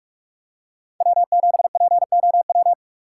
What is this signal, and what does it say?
Morse code, heard as a keyed tone, about 46 wpm, 700 Hz. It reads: W7POW
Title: USA Call Signs - 5 Characters - 6x Character Spacing - 45wpm